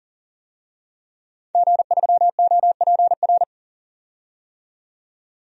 Morse code, heard as a keyed tone, about 40 wpm, 700 Hz. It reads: G3OPR
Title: DXCC-100 Call Signs - 5 Characters - 40wpm